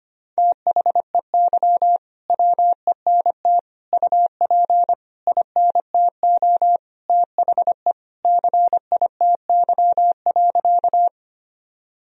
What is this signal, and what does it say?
Morse code, heard as a keyed tone, about 25 wpm, 700 Hz. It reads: THEY WENT UP INTO THE CITY.
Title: Sentences from Top 1000 Words - Review Part 1 - 25wpm